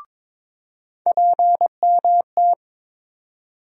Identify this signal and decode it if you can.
Morse code, heard as a keyed tone, about 22 wpm, 700 Hz. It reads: PMT
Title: USA Call Sign Postfixes - 3 Letters 22wpm